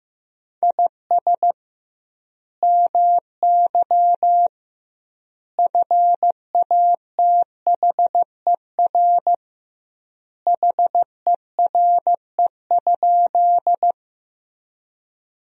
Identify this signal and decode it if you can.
Morse code, heard as a keyed tone, about 15 wpm, 700 Hz. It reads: IS MY FATHER HERE?
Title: 2x Word-Spacing - Sentences from Top 300 Words - 15wpm